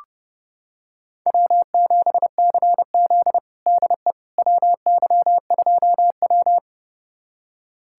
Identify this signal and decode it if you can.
Morse code, heard as a keyed tone, about 30 wpm, 700 Hz. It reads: W7CZ DE WY2W
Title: USA Callsign DE Callsign - 4 Characters - 30wpm